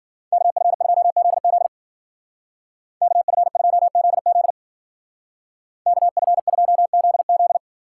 Morse code, heard as a keyed tone, about 49 wpm, 700 Hz. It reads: KF2ZZ KF2ZZ KF2ZZ
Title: Speed Racing - USA 2x2 Call Signs 40wpm